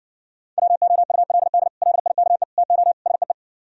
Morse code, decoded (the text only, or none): WOULD LIKE TO SEE